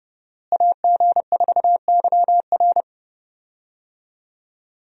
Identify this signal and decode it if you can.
Morse code, heard as a keyed tone, about 30 wpm, 700 Hz. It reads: AG4YR
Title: USA Call Signs - 5 Characters - Repeated 2x - 30wpm